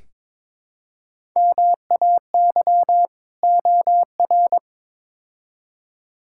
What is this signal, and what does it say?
Morse code, heard as a keyed tone, about 21 wpm, 700 Hz. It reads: MAY OR
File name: practice / 22wpm / Build A Phrase - 5-Word Phrases - 22wpm